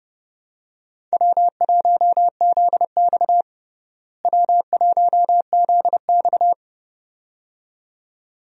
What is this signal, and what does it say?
Morse code, heard as a keyed tone, about 30 wpm, 700 Hz. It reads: W1ZX W1ZX
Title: USA Call Signs - 4 Characters - Repeated 2x - 30wpm